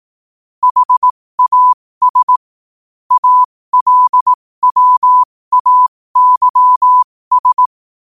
Morse code, heard as a keyed tone, about 18 wpm, spaced as for 15 wpm, 1000 Hz. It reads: HAS ALWAYS
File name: qotdfiles / 15wpm